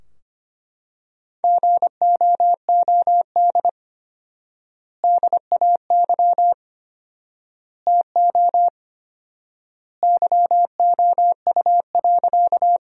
Morse code, heard as a keyed tone, about 25 wpm, 700 Hz. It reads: GOOD DAY TO YOU.
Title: 4x Word-Spacing - Sentences from Top 200 Words - 25wpm